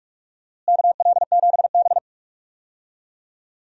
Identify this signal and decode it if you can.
Morse code, heard as a keyed tone, about 41 wpm, 700 Hz. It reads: KL7B
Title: DXCC-100 Call Signs - 4 Characters - Repeated 3x - 45wpm